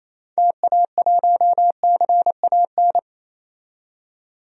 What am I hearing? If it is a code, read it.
Morse code, heard as a keyed tone, about 28 wpm, 700 Hz. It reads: TA1CAN
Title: DXCC-100 Call Signs - 6 Characters - 28wpm